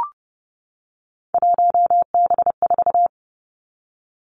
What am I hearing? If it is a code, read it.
Morse code, heard as a keyed tone, about 31 wpm, 700 Hz. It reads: 1B4
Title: Three Challenging Letter-Number Combinations 30wpm